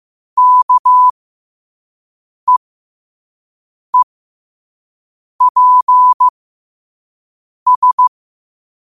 Morse code, heard as a keyed tone, about 15 wpm, spaced as for 5 wpm, 1000 Hz. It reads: KEEPS
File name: qotdfiles / 5wpm